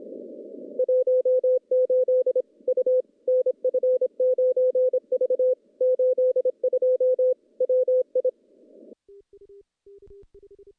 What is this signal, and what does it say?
Morse code, heard as a keyed tone, about 26 wpm, 510 Hz. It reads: 18 U NF9V 82 WI
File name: contests / CW-SS-k5tr- -0400